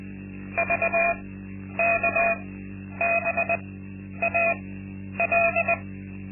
Morse code, heard as a keyed tone, about 20 wpm, spaced as for 10 wpm, 680 Hz. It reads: VKBAL
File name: Linksmakalnis-2